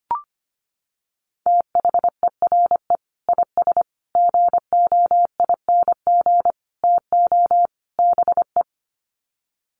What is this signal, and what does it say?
Morse code, heard as a keyed tone, about 25 wpm, 700 Hz. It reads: THERE IS GOING TO BE
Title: Sets of 5 Words Encore 25wpm